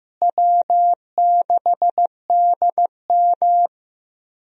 Morse code, heard as a keyed tone, about 15 wpm, 700 Hz. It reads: W6DM
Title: USA Call Signs - 4 Characters - Repeated 3x - 15wpm